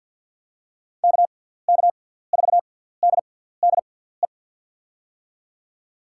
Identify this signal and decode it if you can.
Morse code, heard as a keyed tone, about 46 wpm, spaced as for 17 wpm, 700 Hz. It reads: KK4DDE
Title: USA Call Signs - 6 Characters - 6x Character Spacing - 50wpm